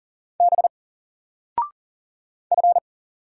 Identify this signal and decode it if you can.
Morse code, heard as a keyed tone, about 40 wpm, 700 Hz. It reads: B F
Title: Single Character (Letter:Number:Punctuation:Prosign) 40wpm